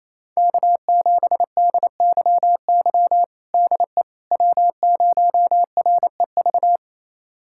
Morse code, heard as a keyed tone, about 28 wpm, 700 Hz. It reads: K7DYY DE W0REV
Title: USA Callsign DE Callsign - 5 Characters - 28wpm